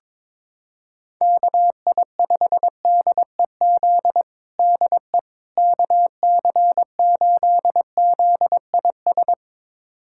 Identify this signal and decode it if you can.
Morse code, heard as a keyed tone, about 21 wpm, 700 Hz. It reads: KI5DEZ DE KC8ZIS
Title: USA Callsign DE Callsign 22wpm